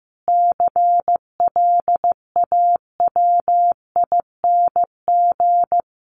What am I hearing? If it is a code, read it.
Morse code, heard as a keyed tone, about 15 wpm, 700 Hz. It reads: CLAWING
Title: A Sound Of Thunder 15wpm